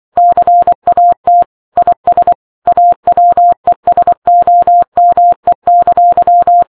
Morse code, heard as a keyed tone, about 24 wpm, 700 Hz. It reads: CAT IS AWESOME!
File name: games / RadwinIsland / Morse Code